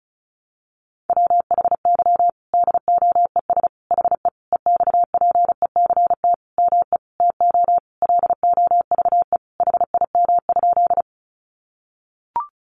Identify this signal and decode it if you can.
Morse code, heard as a keyed tone, about 34 wpm, 700 Hz. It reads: WHY DOES HE EXPECT ME TO LOVE HIM?
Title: Sentences from Top 1100 Words 35wpm